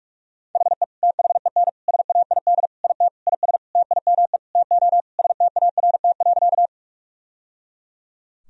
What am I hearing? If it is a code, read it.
Morse code, heard as a keyed tone, about 41 wpm, 700 Hz. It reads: HE THEN SAID IT IS TIME TO START.
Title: Sentences from Top 1000 Words - Review Part 1 - 45wpm